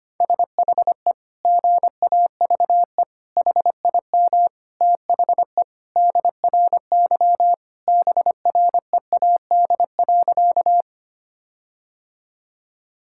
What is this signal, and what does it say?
Morse code, heard as a keyed tone, about 25 wpm, 700 Hz. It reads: SHE GAVE HIM THE DRY BREAD.
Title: Sentences from Top 1600 Words 25wpm